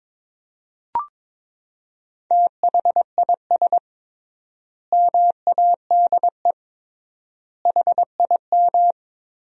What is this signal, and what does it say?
Morse code, heard as a keyed tone, about 22 wpm, 700 Hz. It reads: THIS MADE HIM
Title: 3x Word-Spacing - Sentences from Top 500 Words - 22wpm